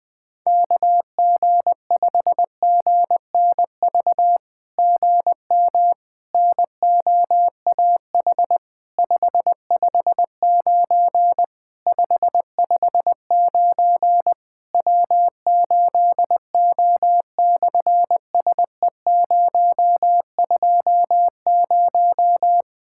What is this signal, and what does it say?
Morse code, heard as a keyed tone, about 20 wpm, 700 Hz. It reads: KG5GNV GM NOAH 559 559 W8O/SE020
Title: SOTA QSO Practice 20wpm